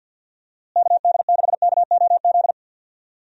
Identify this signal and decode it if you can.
Morse code, heard as a keyed tone, about 48 wpm, 700 Hz. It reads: KD6KOZ
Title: USA Call Signs - 6 Characters - Repeated 2x - 50wpm